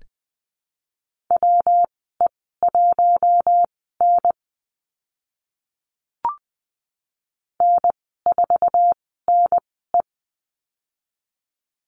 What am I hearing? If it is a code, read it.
Morse code, heard as a keyed tone, about 20 wpm, spaced as for 14 wpm, 700 Hz. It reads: WE1N N4NE
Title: 179 Lesson 29g - Slash - Call Signs 20wpm